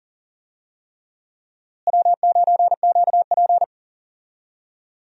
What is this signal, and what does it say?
Morse code, heard as a keyed tone, about 40 wpm, 700 Hz. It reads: W9QP
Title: USA Call Signs - 4 Characters - Repeated 3x - 40wpm